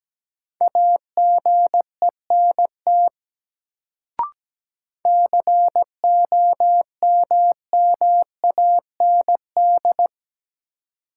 Morse code, heard as a keyed tone, about 17 wpm, 700 Hz. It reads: AGENT COMMAND
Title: Sentences from Top 1600 Words - New Word Drill - 17wpm